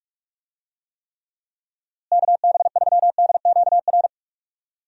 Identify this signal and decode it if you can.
Morse code, heard as a keyed tone, about 45 wpm, 700 Hz. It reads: KB3DQR
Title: USA Call Signs - 6 Characters - Repeated 3x - 45wpm